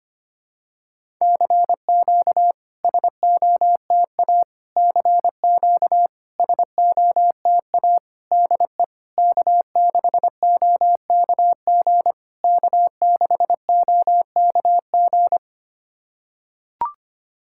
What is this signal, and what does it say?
Morse code, heard as a keyed tone, about 25 wpm, 700 Hz. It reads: CQ SOTA CQ SOTA DE K6OKG K6OKG
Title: SOTA QSO Practice 25wpm